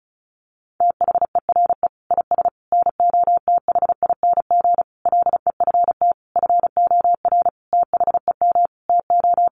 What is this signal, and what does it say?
Morse code, heard as a keyed tone, about 34 wpm, 700 Hz. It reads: THERE IS NOTHING LEFT FOR THEM TO
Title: Sentences from Top 1100 Words 35wpm